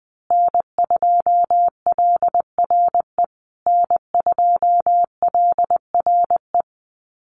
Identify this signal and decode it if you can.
Morse code, heard as a keyed tone, about 19 wpm, 700 Hz. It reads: N2LRE N2LRE
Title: USA Call Signs Repeated 3x 20wpm